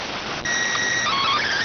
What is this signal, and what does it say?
SSTV (slow-scan television), PD 290.